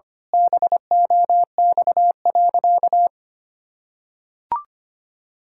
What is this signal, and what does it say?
Morse code, heard as a keyed tone, about 25 wpm, 700 Hz. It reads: BOX.
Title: Pangram Challenge v2 - 25wpm